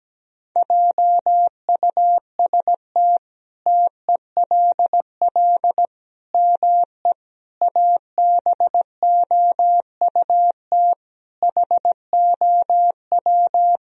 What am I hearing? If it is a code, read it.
Morse code, heard as a keyed tone, about 17 wpm, 700 Hz. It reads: JUST TELL ME ABOUT HOW
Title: Sentences from Top 1000 Words - Review Part 1 - 17wpm